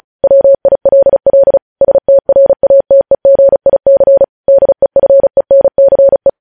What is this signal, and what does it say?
Morse code, heard as a keyed tone, about 36 wpm, 550 Hz. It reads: WILL STRATEGIC DEFENCE
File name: podcast / News headlines-1x-35x35